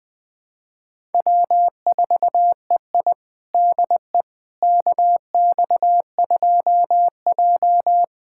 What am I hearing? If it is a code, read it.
Morse code, heard as a keyed tone, about 21 wpm, 700 Hz. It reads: W4EI DE KX2J
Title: USA Callsign DE Callsign - 4 Characters - 20wpm